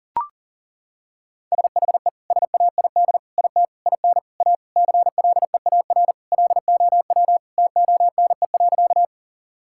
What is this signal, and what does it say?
Morse code, heard as a keyed tone, about 37 wpm, 700 Hz. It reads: SHE SAID IT IN A CLEAR LOW TONE.
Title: Sentences from Top 1600 Words 40wpm